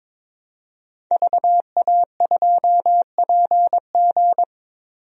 Morse code, heard as a keyed tone, about 22 wpm, 700 Hz. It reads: VA2PG
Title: DXCC-100 Call Signs - 5 Characters - Repeated 3x - 22wpm